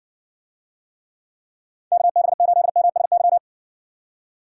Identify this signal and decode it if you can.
Morse code, heard as a keyed tone, about 60 wpm, 700 Hz. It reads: KB8MSY